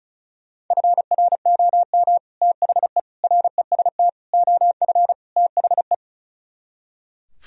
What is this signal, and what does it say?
Morse code, heard as a keyed tone, about 45 wpm, 700 Hz. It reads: FROM THE REST OF THE